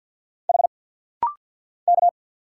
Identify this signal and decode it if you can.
Morse code, heard as a keyed tone, about 49 wpm, 700 Hz. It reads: H K
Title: Single Letter-Number - Rapid-Fire 50wpm